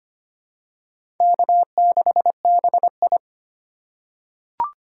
Morse code, heard as a keyed tone, about 25 wpm, 700 Hz. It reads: K6BI
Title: USA Call Signs - 4 Characters - 3x Character Spacing - 25wpm